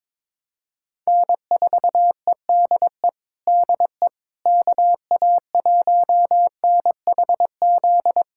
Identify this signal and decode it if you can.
Morse code, heard as a keyed tone, about 22 wpm, 700 Hz. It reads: N4EDE DE KA1NHZ
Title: USA Callsign DE Callsign 22wpm